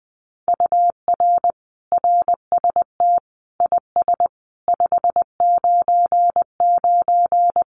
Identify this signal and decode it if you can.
Morse code, heard as a keyed tone, about 19 wpm, 700 Hz. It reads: UR RST IS 599
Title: Common QSO Phrases Encore 20wpm